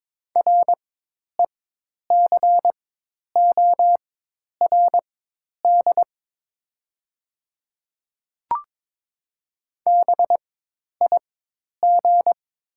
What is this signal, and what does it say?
Morse code, heard as a keyed tone, about 22 wpm, spaced as for 10 wpm, 700 Hz. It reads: RECORD BIG
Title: 4x Character Spacing - Top 500 Words - 22wpm